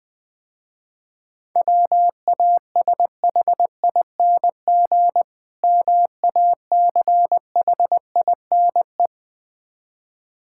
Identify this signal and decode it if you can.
Morse code, heard as a keyed tone, about 20 wpm, 700 Hz. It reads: WASHING MACHINE
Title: Difficult Words - Speed-Racing with Hints - 20wpm